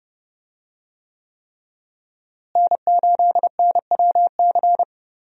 Morse code, heard as a keyed tone, about 31 wpm, 700 Hz. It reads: N8NWC
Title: USA Call Signs - 5 Characters - Repeated 3x - 30wpm